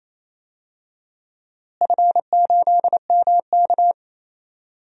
Morse code, heard as a keyed tone, about 27 wpm, 700 Hz. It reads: F8MK
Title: DXCC-100 Call Signs - 4 Characters - Repeated 2x - 28wpm